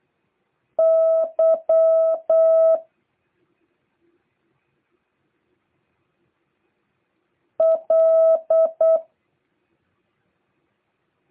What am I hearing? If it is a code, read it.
Morse code, heard as a keyed tone, about 8 wpm, 650 Hz. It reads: Y L